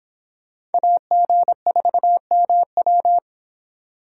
Morse code, heard as a keyed tone, about 31 wpm, 700 Hz. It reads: AG4MW